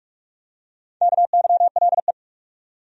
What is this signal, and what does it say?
Morse code, heard as a keyed tone, about 43 wpm, 700 Hz. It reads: KYLE
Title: USA Common Names - Max 5 Letters - 45wpm